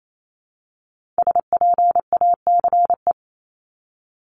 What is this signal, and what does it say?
Morse code, heard as a keyed tone, about 28 wpm, 700 Hz. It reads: SPACE
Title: Multi-letter Phonemes 28wpm